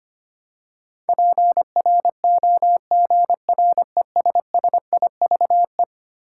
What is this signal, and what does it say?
Morse code, heard as a keyed tone, about 25 wpm, 700 Hz. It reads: PROGRESSIVE
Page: morse-code-ninja-practice-25wpm